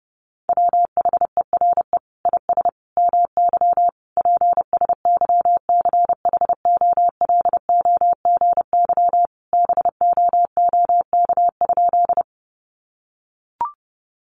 Morse code, heard as a keyed tone, about 30 wpm, 700 Hz. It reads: WHERE IS MY PSYCHOLOGY BOOK?
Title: Sentences from Top 1400 Words 30wpm